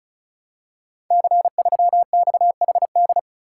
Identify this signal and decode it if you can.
Morse code, heard as a keyed tone, about 36 wpm, 690 Hz. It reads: C3XHD
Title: Code Groups - 5 Letter:Numbers - 35wpm